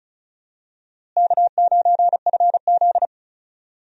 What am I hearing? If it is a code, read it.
Morse code, heard as a keyed tone, about 36 wpm, 700 Hz. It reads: K9FZ
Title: USA Call Signs - 4 Characters - Repeated 2x - 35wpm